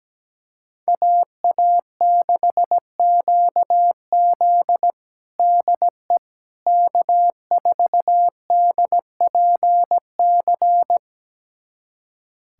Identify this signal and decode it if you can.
Morse code, heard as a keyed tone, about 17 wpm, 700 Hz. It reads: AA6QZ DE K4DPC
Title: USA Callsign DE Callsign - 5 Characters - 17wpm